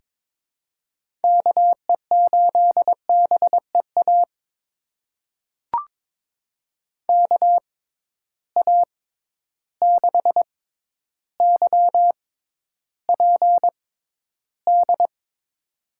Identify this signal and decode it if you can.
Morse code, heard as a keyed tone, about 21 wpm, spaced as for 11 wpm, 700 Hz. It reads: KE8BEA KA6YPD
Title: USA Call Signs - 6 Characters - 6x Character Spacing - 22wpm